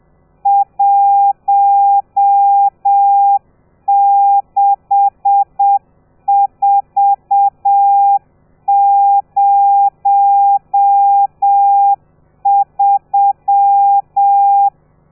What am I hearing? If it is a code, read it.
Morse code, heard as a keyed tone, about 7 wpm, 800 Hz. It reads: 16403
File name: telegrafia / i2tfj / Lez12 30